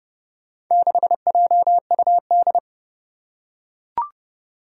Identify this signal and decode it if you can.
Morse code, heard as a keyed tone, about 30 wpm, 700 Hz. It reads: 6JUD
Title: Four Challenging Letter-Number Combinations 30wpm